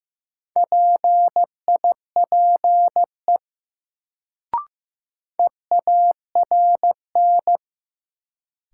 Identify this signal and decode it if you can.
Morse code, heard as a keyed tone, about 15 wpm, 700 Hz. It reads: PIPE EARN
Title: Sentences from Top 1500 Words - New Word Drill - 15wpm